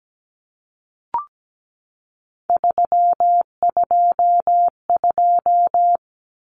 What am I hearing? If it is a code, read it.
Morse code, heard as a keyed tone, about 17 wpm, 700 Hz. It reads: 322
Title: Numbers - 3-Digit 17wpm